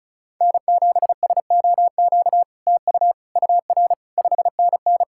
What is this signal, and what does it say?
Morse code, heard as a keyed tone, about 37 wpm, 700 Hz. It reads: N7SOQ TU UR 5NN
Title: SOTA QSO Practice 35wpm